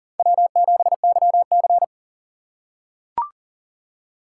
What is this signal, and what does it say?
Morse code, heard as a keyed tone, about 40 wpm, 700 Hz. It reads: W7YC
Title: USA Call Signs - 4 Characters - 6x Character Spacing - 40wpm